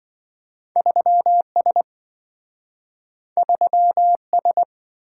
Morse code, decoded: 3S 3S

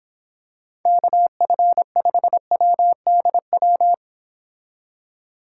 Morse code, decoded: KF5WDW

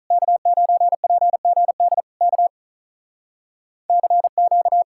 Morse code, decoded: K9PGD K CQ

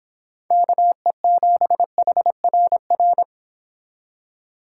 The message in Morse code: KE7HRR

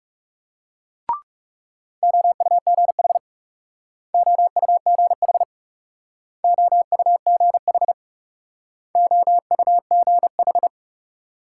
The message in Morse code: OUGH OUGH OUGH OUGH

